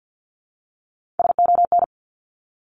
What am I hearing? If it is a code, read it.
Morse code, heard as a keyed tone, about 49 wpm, 700 Hz. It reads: SON
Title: Top 1751-2000 Words 50wpm